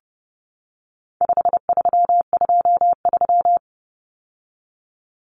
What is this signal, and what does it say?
Morse code, heard as a keyed tone, about 30 wpm, 700 Hz. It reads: 5323
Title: Numbers - 4-Digit 30wpm